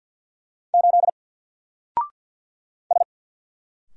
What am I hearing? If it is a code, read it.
Morse code, heard as a keyed tone, about 48 wpm, 700 Hz. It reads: 8 S
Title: Single Letter-Number Alt 50wpm